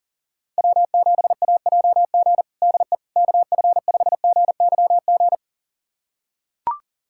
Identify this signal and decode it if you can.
Morse code, heard as a keyed tone, about 40 wpm, 700 Hz. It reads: W7AJG DE KF5GYG
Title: USA Callsign DE Callsign 40wpm